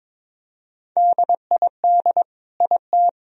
Morse code, heard as a keyed tone, about 22 wpm, 700 Hz. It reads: DID IT
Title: Sentences from Top 1000 Words - Review Part 1 - 22wpm